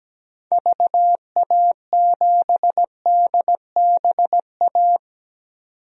Morse code, heard as a keyed tone, about 17 wpm, 700 Hz. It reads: VA7DBA
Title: DXCC-100 Call Signs - 6 Characters - 17wpm